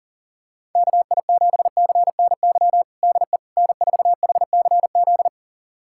Morse code, heard as a keyed tone, about 40 wpm, 700 Hz. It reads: KI7CNY DE N4HCZ
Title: USA Callsign DE Callsign 40wpm